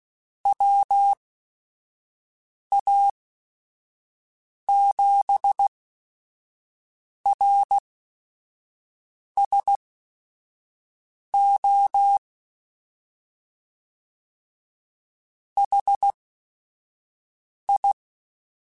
Morse code, heard as a keyed tone, about 16 wpm, spaced as for 5 wpm, 800 Hz. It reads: WA7RSO HI